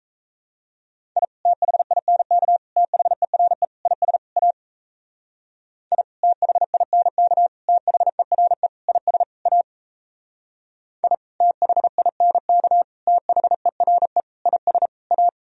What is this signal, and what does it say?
Morse code, heard as a keyed tone, about 35 wpm, 700 Hz. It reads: I THINK THERE IS A I THINK THERE IS A I THINK THERE IS A
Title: Speed-Racing - Sets of 5 Words 28wpm